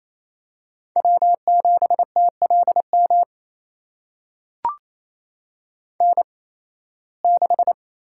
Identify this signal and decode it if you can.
Morse code, heard as a keyed tone, about 28 wpm, spaced as for 18 wpm, 700 Hz. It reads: W7TLM N6
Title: USA Call Signs - 8x Character Spacing - 28wpm